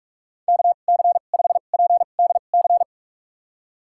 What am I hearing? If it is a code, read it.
Morse code, heard as a keyed tone, about 46 wpm, spaced as for 33 wpm, 700 Hz. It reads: KC5PDC
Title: USA Call Signs - 2x Character Spacing - 45wpm